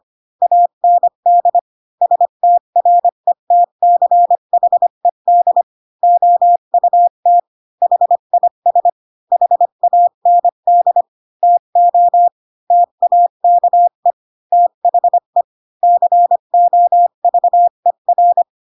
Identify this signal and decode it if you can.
Morse code, heard as a keyed tone, about 25 wpm, 700 Hz. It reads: AND STRETCHED OUT HIS HAND TO TAKE THE COVER
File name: Pinocchio / Pinocchio ch5 20 25 ik1hge